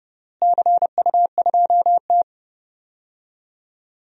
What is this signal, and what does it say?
Morse code, heard as a keyed tone, about 30 wpm, 700 Hz. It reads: CU2T
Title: DXCC-100 Call Signs - 4 Characters - Repeated 3x - 30wpm